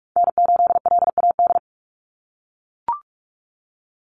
Morse code, heard as a keyed tone, about 44 wpm, 700 Hz. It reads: N8LAD
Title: USA Call Signs 45wpm